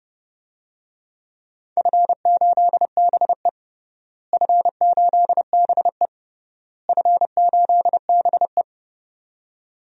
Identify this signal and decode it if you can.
Morse code, heard as a keyed tone, about 30 wpm, 700 Hz. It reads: F8BE F8BE F8BE
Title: DXCC-100 Call Signs - 4 Characters - Repeated 3x - 30wpm